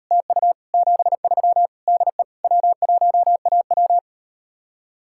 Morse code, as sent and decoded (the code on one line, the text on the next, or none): - ..- / --... ...-- / -.. / .-- .---- .- .--
TU 73 DE W1AW